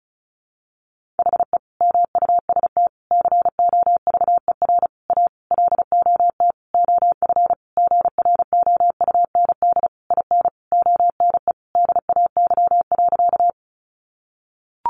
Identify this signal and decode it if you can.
Morse code, heard as a keyed tone, about 34 wpm, 700 Hz. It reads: HE MUST COVER A LOT OF GROUND IN ONE DAY.
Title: Sentences from Top 500 Words 35wpm